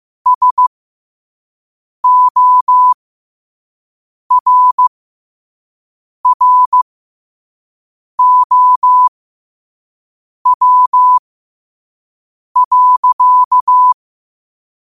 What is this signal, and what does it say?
Morse code, heard as a keyed tone, about 15 wpm, spaced as for 5 wpm, 1000 Hz. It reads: SORROW.
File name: qotdfiles / 5wpm